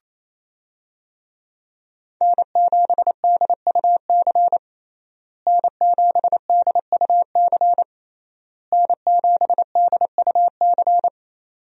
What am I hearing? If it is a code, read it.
Morse code, heard as a keyed tone, about 28 wpm, 700 Hz. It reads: N7DUC N7DUC N7DUC
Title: USA Call Signs - 5 Characters - Repeated 3x - 28wpm